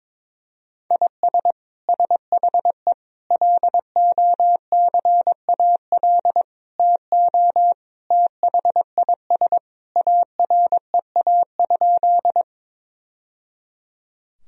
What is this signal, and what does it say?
Morse code, heard as a keyed tone, about 22 wpm, 700 Hz. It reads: IS SHE LOCAL TO THIS AREA?
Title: Sentences from Top 1000 Words - Review Part 3 - 22wpm